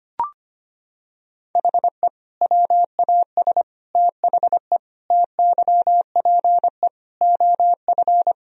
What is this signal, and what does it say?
Morse code, heard as a keyed tone, about 25 wpm, 700 Hz. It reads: HE WAS THE TYPE OF
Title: Sentences from Top 1000 Words - Review Part 3 - 25wpm